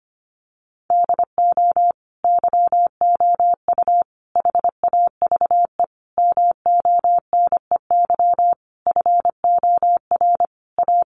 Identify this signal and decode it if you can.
Morse code, heard as a keyed tone, about 25 wpm, 700 Hz. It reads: DO YOU HAVE MONEY FOR A
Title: Sentences from Top 1400 Words 25wpm